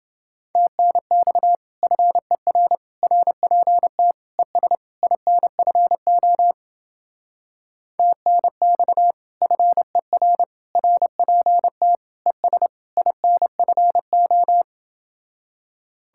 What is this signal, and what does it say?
Morse code, heard as a keyed tone, about 29 wpm, 700 Hz. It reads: TNX FER RPT ES INFO TNX FER RPT ES INFO